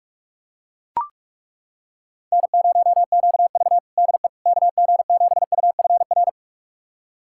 Morse code, heard as a keyed tone, about 41 wpm, 700 Hz. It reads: N0QV DE KG7UFR
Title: USA Callsign DE Callsign 45wpm